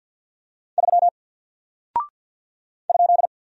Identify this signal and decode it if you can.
Morse code, heard as a keyed tone, about 48 wpm, 700 Hz. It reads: 3 ?
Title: Single Character (Letter:Number:Punctuation:Prosign) Alt 50wpm